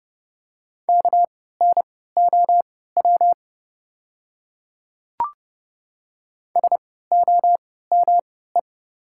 Morse code, heard as a keyed tone, about 30 wpm, spaced as for 17 wpm, 700 Hz. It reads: KNOW SOME
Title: 3x Character Spacing - Top 100 Words - 30wpm